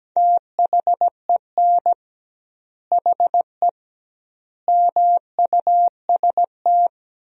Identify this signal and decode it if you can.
Morse code, heard as a keyed tone, about 17 wpm, 700 Hz. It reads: THEN HE MUST
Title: 2x Word-Spacing - Sentences from Top 500 Words - 17wpm